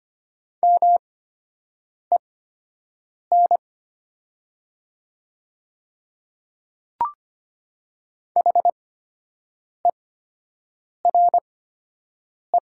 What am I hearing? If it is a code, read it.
Morse code, heard as a keyed tone, about 25 wpm, spaced as for 7 wpm, 700 Hz. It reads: MEN HERE
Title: 8x Character Spacing - Top 200 Words - 25wpm